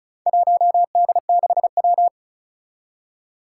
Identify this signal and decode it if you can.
Morse code, heard as a keyed tone, about 36 wpm, 700 Hz. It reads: 1D6W
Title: Four Challenging Letter-Number Combinations 35wpm